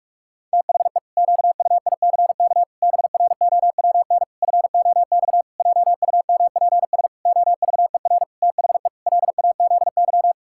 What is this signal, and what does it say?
Morse code, heard as a keyed tone, about 46 wpm, 700 Hz. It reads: THE QUICK BROWN FOX JUMPS OVER THE LAZY
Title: Pangram Challenge v2 - 45wpm